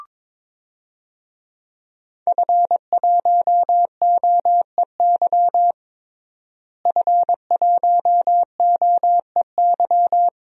Morse code, heard as a keyed tone, about 22 wpm, 700 Hz. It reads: F1OEY F1OEY
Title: DXCC-100 Call Signs - 5 Characters - Repeated 3x - 22wpm